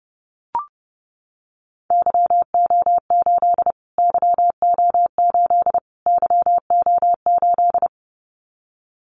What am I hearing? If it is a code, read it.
Morse code, heard as a keyed tone, about 30 wpm, 700 Hz. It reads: YO8 YO8 YO8
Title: Callsign Prefixes for Top 25 DXCC - Repeated 3x - 30wpm